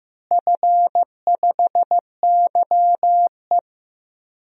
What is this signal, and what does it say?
Morse code, heard as a keyed tone, about 15 wpm, 700 Hz. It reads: F5YE
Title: DXCC-100 Call Signs - 4 Characters - Repeated 3x - 15wpm